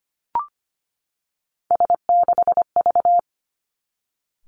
Morse code, heard as a keyed tone, about 25 wpm, 700 Hz. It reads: S6V
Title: Three Challenging Letter-Number Combinations 25wpm